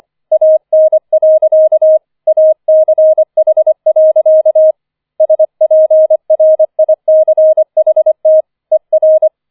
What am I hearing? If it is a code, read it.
Morse code, heard as a keyed tone, about 25 wpm, 630 Hz. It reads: AN. ACH. SPRICHT ER
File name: 004 Vierter Streich 25wpm